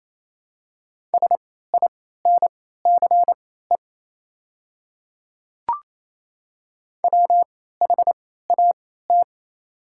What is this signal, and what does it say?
Morse code, heard as a keyed tone, about 28 wpm, spaced as for 16 wpm, 700 Hz. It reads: SINCE WHAT
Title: 3x Character Spacing - Top 300 Words - 28wpm